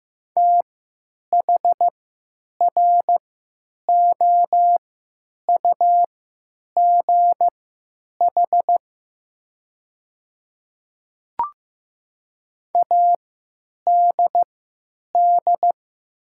Morse code, heard as a keyed tone, about 15 wpm, spaced as for 9 wpm, 700 Hz. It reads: THROUGH ADD
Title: 3x Character Spacing - Top 200 Words - 15wpm